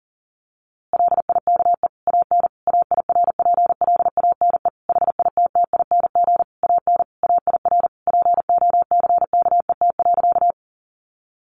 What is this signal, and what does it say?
Morse code, heard as a keyed tone, about 40 wpm, 700 Hz. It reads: LIKE AN AIRPLANE HITTING AN AIR POCKET.
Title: A Sound Of Thunder 40wpm